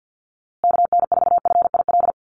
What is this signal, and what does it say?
Morse code, heard as a keyed tone, about 50 wpm, 700 Hz. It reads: KN4FIL
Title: USA Call Signs Repeated 3x 50wpm